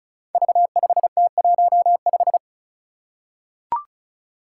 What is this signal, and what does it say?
Morse code, heard as a keyed tone, about 36 wpm, 700 Hz. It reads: V5T15
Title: Five Challenging Letter-Number Combinations 35wpm